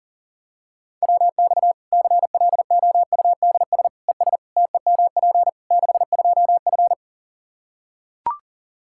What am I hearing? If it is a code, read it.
Morse code, heard as a keyed tone, about 40 wpm, 700 Hz. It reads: WX CLOUDS ES TEMP 62F
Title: K1USN SST QSO Practice 40wpm